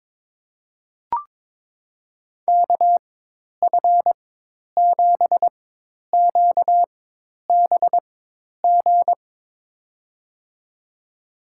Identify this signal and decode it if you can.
Morse code, heard as a keyed tone, about 22 wpm, spaced as for 10 wpm, 700 Hz. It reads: KF7QBG
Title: USA Call Signs - 6 Characters - 4x Character Spacing - 22wpm